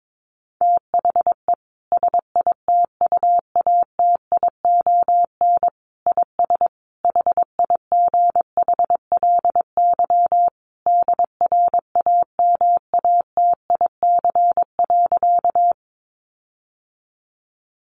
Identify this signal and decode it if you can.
Morse code, heard as a keyed tone, about 22 wpm, 700 Hz. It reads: THE SITUATION IS HIGHLY DRAMATIC.
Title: Sentences from Top 1300 Words 22wpm